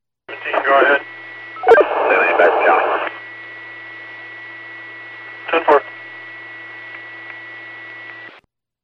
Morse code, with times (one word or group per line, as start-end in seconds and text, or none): none